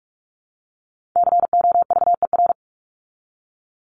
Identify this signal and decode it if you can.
Morse code, heard as a keyed tone, about 44 wpm, 700 Hz. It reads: COVER
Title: ED Suffix - Top 100 Words with Base Words 45wpm